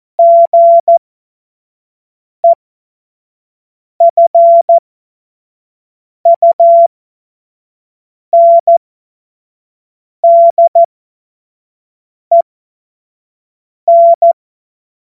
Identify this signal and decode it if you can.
Morse code, heard as a keyed tone, about 14 wpm, spaced as for 5 wpm, 680 Hz. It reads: GEFUNDEN
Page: cw-witze